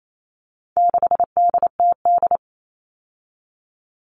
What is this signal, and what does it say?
Morse code, heard as a keyed tone, about 28 wpm, 700 Hz. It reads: 6DTD
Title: All Cut Numbers - 4-Digit 28wpm